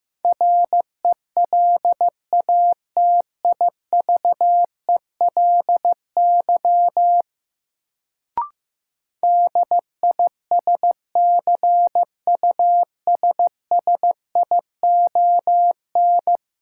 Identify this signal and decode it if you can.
Morse code, heard as a keyed tone, about 15 wpm, 700 Hz. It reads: RELATIVELY DISCUSSION
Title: Sentences from Top 1100 Words - New Word Drill - 15wpm